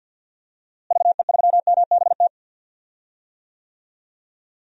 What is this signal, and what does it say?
Morse code, heard as a keyed tone, about 53 wpm, 700 Hz. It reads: VE3MBT
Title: DXCC-100 Call Signs - 6 Characters - Repeated 3x - 50wpm